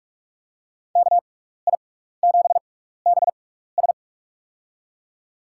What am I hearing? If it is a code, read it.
Morse code, heard as a keyed tone, about 43 wpm, spaced as for 15 wpm, 690 Hz. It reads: KI7BS
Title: USA Call Signs - 5 Characters - 6x Character Spacing - 45wpm